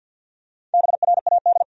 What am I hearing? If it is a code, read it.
Morse code, heard as a keyed tone, about 49 wpm, 700 Hz. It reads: BRAD
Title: USA Common Names - Max 5 Letters - 50wpm